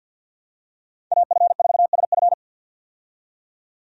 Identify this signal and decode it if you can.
Morse code, heard as a keyed tone, about 52 wpm, 700 Hz. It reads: AF4SF
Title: USA Call Signs - 5 Characters - 50wpm